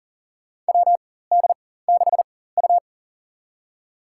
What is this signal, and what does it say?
Morse code, heard as a keyed tone, about 40 wpm, spaced as for 19 wpm, 700 Hz. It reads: WD6U
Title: USA Call Signs - 4 Characters - 4x Character Spacing - 40wpm